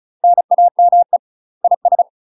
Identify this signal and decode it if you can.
Morse code, heard as a keyed tone, about 34 wpm, 700 Hz. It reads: NAME IS
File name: UK MASTER QSOs TRAINING / 35WPM / UK MASTER QSOs 0025